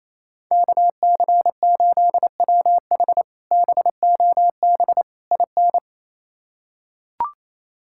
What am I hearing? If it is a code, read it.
Morse code, heard as a keyed tone, about 28 wpm, 700 Hz. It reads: KC8WH BOB IN
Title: K1USN SST QSO Practice - 28wpm